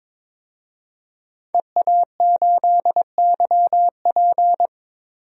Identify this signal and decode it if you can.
Morse code, heard as a keyed tone, about 21 wpm, 700 Hz. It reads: EA8YP
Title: DXCC-100 Call Signs - 5 Characters - Repeated 3x - 22wpm